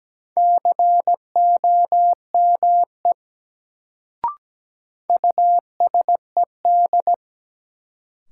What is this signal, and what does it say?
Morse code, heard as a keyed tone, about 17 wpm, 700 Hz. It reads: COME USED
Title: Sets of 2 Words - New Word Drill - 17wpm